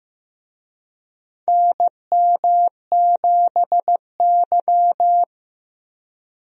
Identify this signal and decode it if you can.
Morse code, heard as a keyed tone, about 15 wpm, 700 Hz. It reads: NM7Y
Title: USA Call Signs - 4 Characters - Repeated 2x - 15wpm